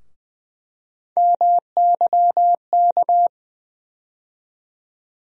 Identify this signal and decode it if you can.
Morse code, heard as a keyed tone, about 20 wpm, 700 Hz. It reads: MYK
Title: USA Call Sign Postfixes - 3 Letters 20wpm